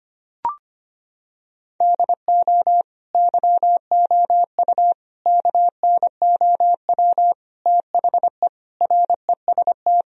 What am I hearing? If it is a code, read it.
Morse code, heard as a keyed tone, about 25 wpm, 700 Hz. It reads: DO YOU KNOW THE REST
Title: Sentences from Top 1000 Words - Review Part 2 - 25wpm